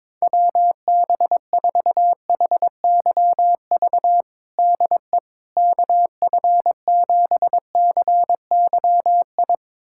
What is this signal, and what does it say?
Morse code, heard as a keyed tone, about 21 wpm, 700 Hz. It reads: WB4HYV DE KF7CYI
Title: USA Callsign DE Callsign - 6 Characters - 22wpm